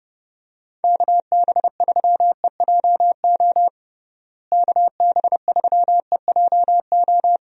Morse code, heard as a keyed tone, about 32 wpm, 700 Hz. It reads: KB3EJO KB3EJO
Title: USA Call Signs - 6 Characters - Repeated 3x - 30wpm